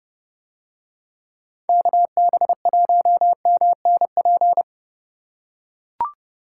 Morse code, heard as a keyed tone, about 29 wpm, 700 Hz. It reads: KB1MNP